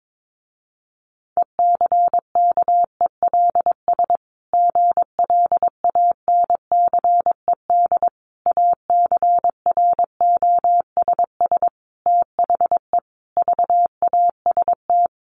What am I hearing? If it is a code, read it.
Morse code, heard as a keyed tone, about 22 wpm, 700 Hz. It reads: ECKELS GLANCED ACROSS THE VAST
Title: A Sound Of Thunder 22wpm